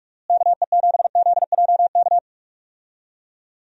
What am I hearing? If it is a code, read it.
Morse code, heard as a keyed tone, about 46 wpm, 700 Hz. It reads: KE7ZJK
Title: USA Call Signs - 6 Characters - Repeated 2x - 45wpm